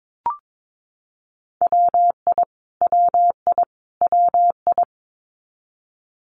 Morse code, heard as a keyed tone, about 22 wpm, 700 Hz. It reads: WI WI WI
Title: US States Repeated 3x 22wpm